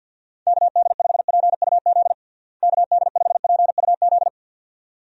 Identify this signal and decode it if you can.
Morse code, heard as a keyed tone, about 50 wpm, 700 Hz. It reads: KD5PUZ KD5PUZ
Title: USA Call Signs - 6 Characters - Repeated 2x - 50wpm